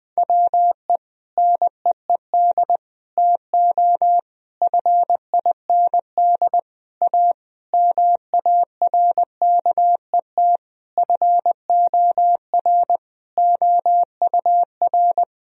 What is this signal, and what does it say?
Morse code, heard as a keyed tone, about 21 wpm, 700 Hz. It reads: WE NEED TO FIND A MARKET FOR OUR
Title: Sentences from Top 1000 Words - Review Part 3 - 20wpm